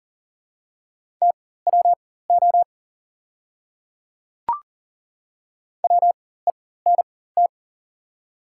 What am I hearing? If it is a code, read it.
Morse code, heard as a keyed tone, about 40 wpm, spaced as for 19 wpm, 690 Hz. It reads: TWO WENT
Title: 4x Character Spacing - Top 300 Words - 40wpm